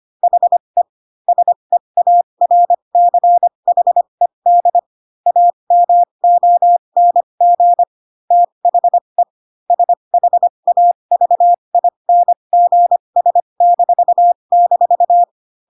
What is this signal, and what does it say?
Morse code, heard as a keyed tone, about 25 wpm, spaced as for 22 wpm, 700 Hz. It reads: HE SEARCHED AMONG THE SHAVINGS--